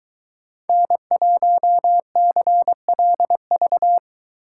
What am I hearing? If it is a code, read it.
Morse code, heard as a keyed tone, about 22 wpm, 700 Hz. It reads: N1CLV